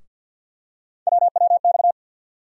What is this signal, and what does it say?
Morse code, heard as a keyed tone, about 49 wpm, 690 Hz. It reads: WWX
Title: USA Call Sign Postfixes - 3 Letters 50wpm